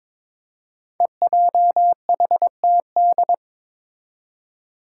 Morse code, heard as a keyed tone, about 21 wpm, 700 Hz. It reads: EJHTD
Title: Five Challenging Letter-Number Combinations 22wpm